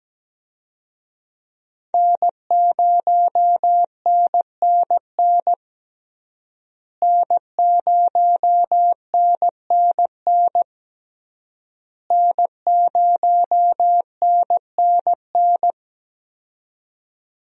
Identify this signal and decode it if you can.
Morse code, heard as a keyed tone, about 17 wpm, 700 Hz. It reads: N0NNN N0NNN N0NNN
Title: USA Call Signs - 5 Characters - Repeated 3x - 17wpm